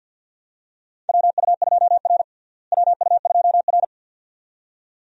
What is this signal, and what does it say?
Morse code, heard as a keyed tone, about 53 wpm, 700 Hz. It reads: WU2R WU2R
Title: USA Call Signs - 4 Characters - Repeated 2x - 50wpm